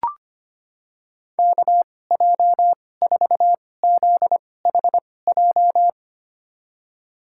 Morse code, heard as a keyed tone, about 25 wpm, spaced as for 18 wpm, 700 Hz. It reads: KJ4ZHJ
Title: USA Call Signs - 6 Characters - 2x Character Spacing - 25wpm